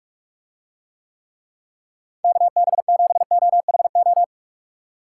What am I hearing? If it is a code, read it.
Morse code, heard as a keyed tone, about 45 wpm, 700 Hz. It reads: KB7OHO